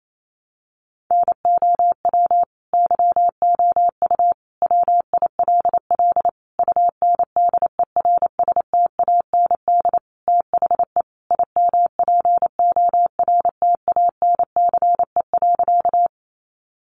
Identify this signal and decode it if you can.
Morse code, heard as a keyed tone, about 28 wpm, 700 Hz. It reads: NOW YOU WILL UNDERSTAND THE IMPORTANCE.
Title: Sentences from Top 1200 Words 28wpm